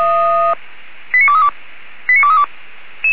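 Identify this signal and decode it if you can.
Five-tone selective call (ZVEI), heard as quick runs of five tones, tones of about 80 ms. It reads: 89232 89232
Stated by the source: ZVEI five-tone selective call received by SDR.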